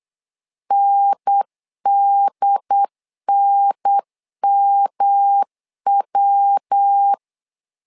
Morse code, heard as a keyed tone, about 8 wpm, 800 Hz. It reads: NDNMW
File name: LEZIONI / 3 MWRO / 3 MW 8wpm 2501